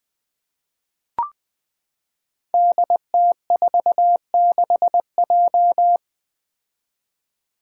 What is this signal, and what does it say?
Morse code, heard as a keyed tone, about 20 wpm, 700 Hz. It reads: DT46J
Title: Five Challenging Letter-Number Combinations 20wpm